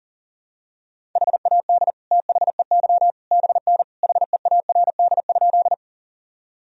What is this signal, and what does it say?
Morse code, heard as a keyed tone, about 40 wpm, 700 Hz. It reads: HAD THEY BN HEARD?
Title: Sentences from Top 400 Words and Abbreviations 40wpm